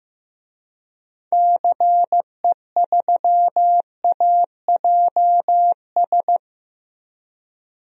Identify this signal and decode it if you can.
Morse code, heard as a keyed tone, about 15 wpm, 700 Hz. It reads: CE3AJS
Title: DXCC-100 Call Signs - 6 Characters - Repeated 3x - 15wpm